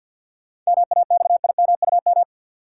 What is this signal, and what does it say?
Morse code, heard as a keyed tone, about 53 wpm, 700 Hz. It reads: MAXIMUM
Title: Sentences from Top 1300 Words - New Word Drill Alt - 50wpm